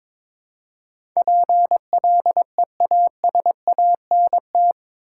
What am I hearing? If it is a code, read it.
Morse code, heard as a keyed tone, about 22 wpm, 700 Hz. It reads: PLEASANT
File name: practice / 22wpm / Eight Letter Words - 22wpm